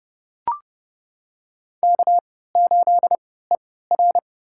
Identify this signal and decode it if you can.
Morse code, heard as a keyed tone, about 30 wpm, spaced as for 17 wpm, 700 Hz. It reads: K8ER
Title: USA Call Signs - 4 Characters - 3x Character Spacing - 30wpm